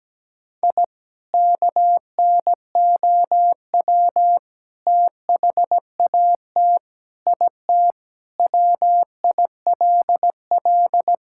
Morse code, heard as a keyed tone, about 17 wpm, 700 Hz. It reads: I KNOW THAT IT WILL
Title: Sentences from Top 1000 Words - Review Part 6 - 17wpm